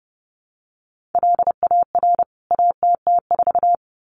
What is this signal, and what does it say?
Morse code, heard as a keyed tone, about 30 wpm, 700 Hz. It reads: LAR ATT4
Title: CWT Exchange 30wpm